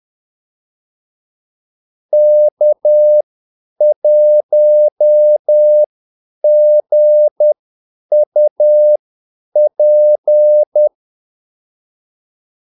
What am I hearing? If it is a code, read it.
Morse code, heard as a keyed tone, about 10 wpm, spaced as for 8 wpm, 600 Hz. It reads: K1GUP
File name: Callsigns-10-8wpm